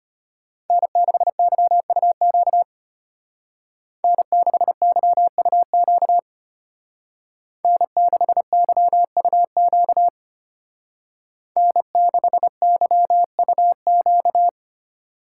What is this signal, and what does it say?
Morse code, heard as a keyed tone, about 32 wpm, 700 Hz. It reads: N6YUQ N6YUQ N6YUQ N6YUQ